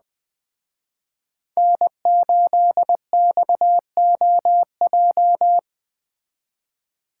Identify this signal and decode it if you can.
Morse code, heard as a keyed tone, about 20 wpm, 700 Hz. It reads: N8XOJ